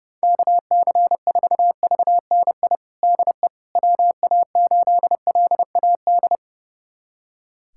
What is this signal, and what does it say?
Morse code, heard as a keyed tone, about 32 wpm, 700 Hz. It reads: KC4VNI DE WA8LAD
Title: USA Callsign DE Callsign 30wpm